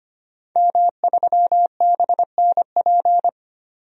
Morse code, heard as a keyed tone, about 25 wpm, 700 Hz. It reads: M3BNP
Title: DXCC-100 Call Signs - 5 Characters - 25wpm